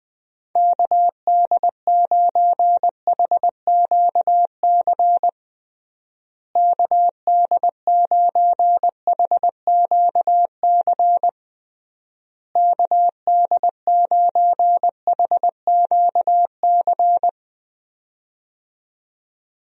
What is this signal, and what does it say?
Morse code, heard as a keyed tone, about 21 wpm, 700 Hz. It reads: KD9HQC KD9HQC KD9HQC
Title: USA Call Signs - 6 Characters - Repeated 3x - 20wpm